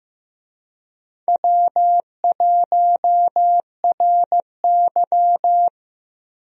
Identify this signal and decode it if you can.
Morse code, heard as a keyed tone, about 15 wpm, 700 Hz. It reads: W1RY
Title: USA Call Signs - 4 Characters - Repeated 3x - 15wpm